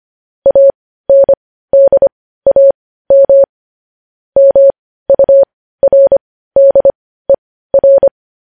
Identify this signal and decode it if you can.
Morse code, heard as a keyed tone, about 25 wpm, spaced as for 15 wpm, 550 Hz. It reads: ANDAM MURDER
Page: news headlines in morse code at 15 wpm